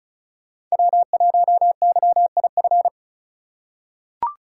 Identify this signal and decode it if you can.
Morse code, heard as a keyed tone, about 34 wpm, 700 Hz. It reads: W1YIF
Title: USA Call Signs - 5 Characters - 4x Character Spacing - 35wpm